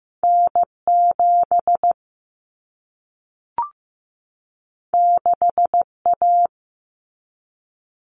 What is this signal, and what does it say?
Morse code, heard as a keyed tone, about 15 wpm, 700 Hz. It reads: N7 6A
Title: Common Cut Numbers - 2-Digit 15wpm